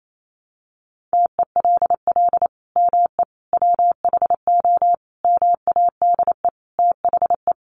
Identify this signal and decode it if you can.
Morse code, heard as a keyed tone, about 28 wpm, 700 Hz. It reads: TELL ME WHO MADE THE
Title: Sentences from Top 200 Words 28wpm